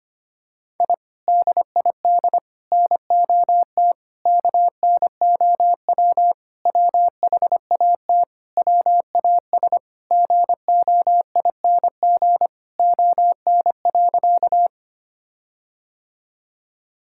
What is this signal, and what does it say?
Morse code, heard as a keyed tone, about 25 wpm, 700 Hz. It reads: I DID NOT KNOW WHAT WAS GOING ON.
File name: practice / 25wpm / Sentences with 4-Word Phrases - 25wpm